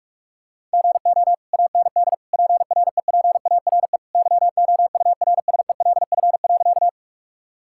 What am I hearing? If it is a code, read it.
Morse code, heard as a keyed tone, about 46 wpm, 700 Hz. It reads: GO AND PREPARE YOURSELF.
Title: Sentences from Top 1000 Words - Review Part 6 - 45wpm